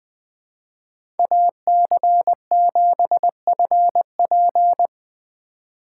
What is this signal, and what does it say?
Morse code, heard as a keyed tone, about 20 wpm, 700 Hz. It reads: AC7FP
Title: USA Call Signs - 5 Characters - 4x Character Spacing - 20wpm